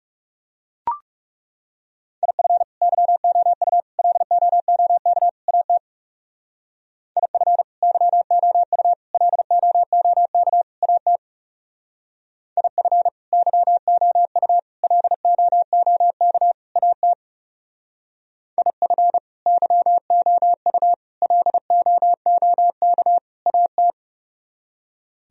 Morse code, heard as a keyed tone, about 36 wpm, 700 Hz. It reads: IF YOU LOOK AT IF YOU LOOK AT IF YOU LOOK AT IF YOU LOOK AT
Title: Speed-Racing - Sets of 4 Words 30wpm